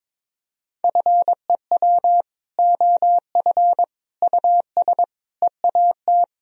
Morse code, heard as a keyed tone, about 21 wpm, 700 Hz. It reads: FEW OF US EAT
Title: Sentences from Top 1000 Words - Review Part 3 - 22wpm